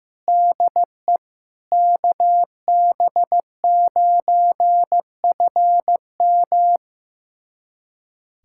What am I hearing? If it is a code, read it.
Morse code, heard as a keyed tone, about 15 wpm, 700 Hz. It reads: DE KB9FM
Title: USA Callsign DE Callsign - 5 Characters - 15wpm